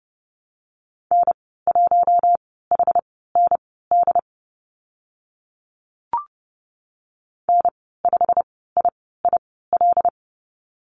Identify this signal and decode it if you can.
Morse code, heard as a keyed tone, about 30 wpm, spaced as for 17 wpm, 700 Hz. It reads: N1HND N5IIL
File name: Lesson 15g - Number 5 - Call Signs 30wpm